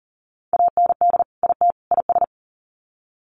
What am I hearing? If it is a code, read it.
Morse code, heard as a keyed tone, about 40 wpm, 700 Hz. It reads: AND IT IS
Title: Sets of 3 Words 40wpm